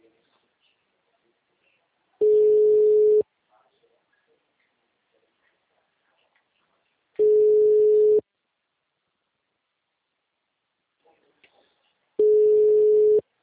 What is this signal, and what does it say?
Ringback tone (Europe, 425 Hz).
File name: out- -2017- - - .20217